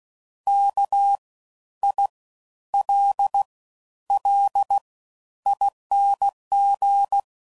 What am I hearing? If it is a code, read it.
Morse code, heard as a keyed tone, about 16 wpm, spaced as for 11 wpm, 800 Hz. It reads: KILLING
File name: Suffixes / Special INGs / Fast Ings